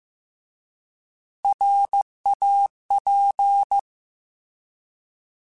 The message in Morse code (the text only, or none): RAP